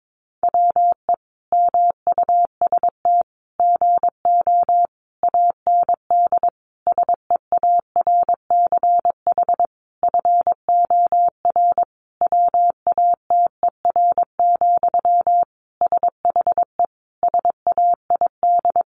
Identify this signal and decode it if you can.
Morse code, heard as a keyed tone, about 22 wpm, 700 Hz. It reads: WE MUST GO AND SEARCH FOR WATER, SHE SAID
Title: Sentences from Top 800 Words 22wpm